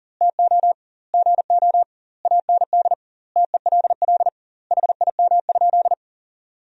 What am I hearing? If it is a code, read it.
Morse code, heard as a keyed tone, about 40 wpm, 700 Hz. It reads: TO GO AND TELL HIM?
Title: 2x Word-Spacing - Sentences from Top 200 Words - 40wpm